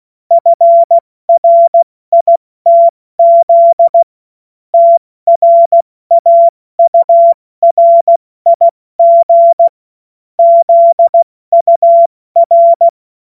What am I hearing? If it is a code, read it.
Morse code, heard as a keyed tone, about 16 wpm, 680 Hz. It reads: FRITZ TRAURIG ZUR
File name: Morsekurs / Witze / Witz 07. September 2021, 70 F80